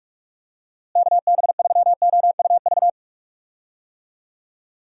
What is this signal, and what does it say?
Morse code, heard as a keyed tone, about 41 wpm, 700 Hz. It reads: KB3OUV
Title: USA Call Signs - 6 Characters - Repeated 2x - 45wpm